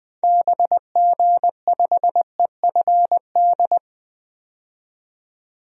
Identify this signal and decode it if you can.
Morse code, heard as a keyed tone, about 21 wpm, 700 Hz. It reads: BG5EFD
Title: DXCC-100 Call Signs - 6 Characters - 20wpm